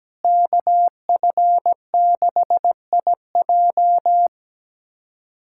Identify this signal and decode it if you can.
Morse code, heard as a keyed tone, about 17 wpm, 700 Hz. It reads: KF6IJ
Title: USA Callsign DE Callsign - 5 Characters - 17wpm